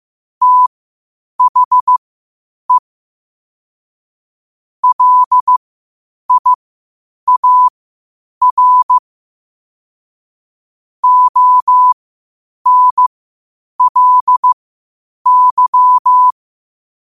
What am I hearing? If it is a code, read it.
Morse code, heard as a keyed tone, about 15 wpm, spaced as for 8 wpm, 1000 Hz. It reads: THE LIAR ONLY
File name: qotdfiles / 8wpm